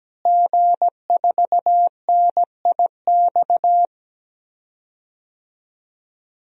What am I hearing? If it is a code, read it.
Morse code, heard as a keyed tone, about 17 wpm, 700 Hz. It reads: G4NIX
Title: DXCC-100 Call Signs - 5 Characters - Repeated 3x - 17wpm